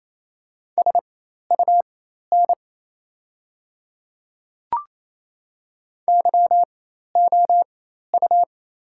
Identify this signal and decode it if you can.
Morse code, heard as a keyed tone, about 28 wpm, spaced as for 13 wpm, 700 Hz. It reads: SUN YOU
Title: 4x Character Spacing - Top 300 Words - 28wpm